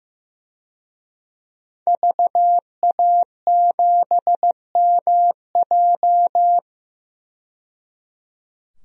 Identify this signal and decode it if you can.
Morse code, heard as a keyed tone, about 15 wpm, 700 Hz. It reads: VA7MJ
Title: DXCC-100 Call Signs - 5 Characters - 15wpm